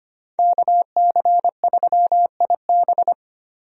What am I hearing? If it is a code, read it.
Morse code, heard as a keyed tone, about 25 wpm, 700 Hz. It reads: KC3IB
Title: USA Call Signs - 5 Characters - Repeated 3x - 25wpm